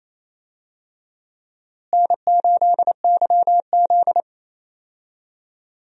Morse code, heard as a keyed tone, about 28 wpm, 700 Hz. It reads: N8YZ
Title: USA Call Signs - 4 Characters - Repeated 3x - 28wpm